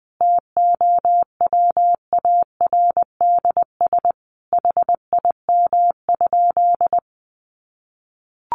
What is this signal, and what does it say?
Morse code, heard as a keyed tone, about 20 wpm, 700 Hz. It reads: TOWARDS HIM?
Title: Sentences from Top 1100 Words 20wpm